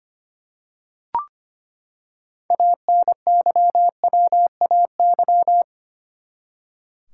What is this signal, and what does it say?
Morse code, heard as a keyed tone, about 25 wpm, 700 Hz. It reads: ANYWAY
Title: Six Letter Words - 25wpm